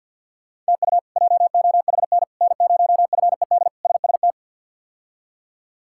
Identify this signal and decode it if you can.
Morse code, heard as a keyed tone, about 49 wpm, 700 Hz. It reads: TU JOHN N0FED SST
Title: K1USN SST QSO Practice - 50wpm